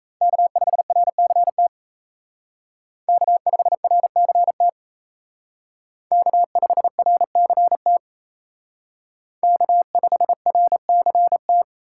Morse code, heard as a keyed tone, about 36 wpm, 700 Hz. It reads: K5RCT K5RCT K5RCT K5RCT